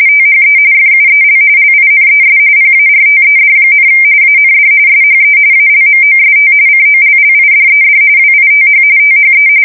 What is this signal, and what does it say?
RTTY, about 45.45 baud. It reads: THE FUEL LINES PARTED, AND SHE ALMOST COLLIDED WITH THE
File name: fsk170 / battleshipnj